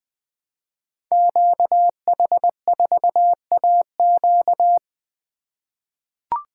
Morse code, heard as a keyed tone, about 20 wpm, 700 Hz. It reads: QH4AQ
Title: Code Groups - 5 Letter:Numbers - 20wpm